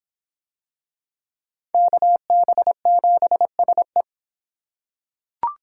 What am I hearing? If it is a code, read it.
Morse code, heard as a keyed tone, about 26 wpm, 700 Hz. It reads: KB7SE